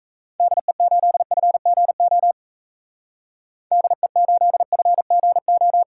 Morse code, heard as a keyed tone, about 34 wpm, 700 Hz. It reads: DE8FGO DE8FGO
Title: Speed Racing - USA 2x3 Call Signs 28wpm